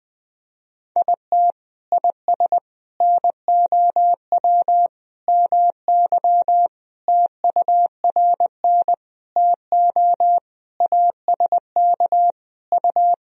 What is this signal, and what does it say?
Morse code, heard as a keyed tone, about 21 wpm, 700 Hz. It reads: IT IS NOW MY TURN TO ASK U
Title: Sentences from Top 200 Words and Abbreviations 20wpm